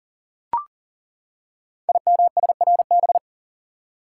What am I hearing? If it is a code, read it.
Morse code, heard as a keyed tone, about 38 wpm, 690 Hz. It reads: IMSRB
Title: Code Groups - 5 Letters - 40wpm